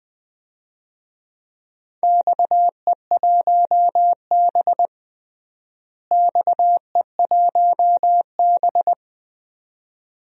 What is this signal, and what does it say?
Morse code, heard as a keyed tone, about 20 wpm, 700 Hz. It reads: XE1B XE1B
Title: DXCC-100 Call Signs - 4 Characters - Repeated 2x - 20wpm